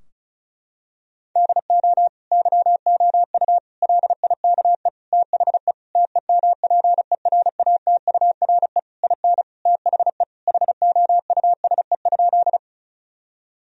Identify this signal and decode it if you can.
Morse code, heard as a keyed tone, about 36 wpm, 700 Hz. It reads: DO YOU LIKE THE TEMPERATURE IN THE HOUSE?
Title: Sentences from Top 1000 Words - Review Part 4 - 35wpm